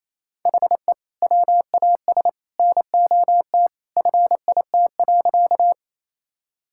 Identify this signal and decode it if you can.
Morse code, heard as a keyed tone, about 28 wpm, 700 Hz. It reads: HE WAS NOT FIT.
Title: Sentences from Top 1000 Words - Review Part 5 - 28wpm